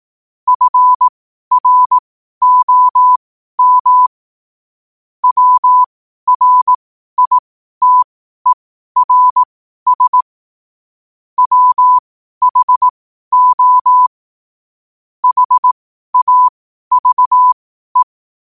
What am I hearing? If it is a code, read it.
Morse code, heard as a keyed tone, about 18 wpm, spaced as for 12 wpm, 1000 Hz. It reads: FROM WRITERS WHO HAVE
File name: qotdfiles / 12wpm